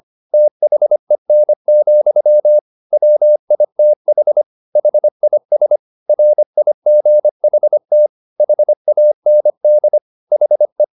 Morse code, heard as a keyed tone, about 25 wpm, 600 Hz. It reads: THEN, WITH HIS RIGHT HAND HE
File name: the old man and the sea part 1 25wpm